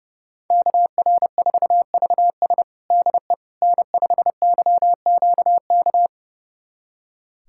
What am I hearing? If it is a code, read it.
Morse code, heard as a keyed tone, about 30 wpm, 700 Hz. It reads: KR4VS DE N5YQK
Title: USA Callsign DE Callsign - 5 Characters - 30wpm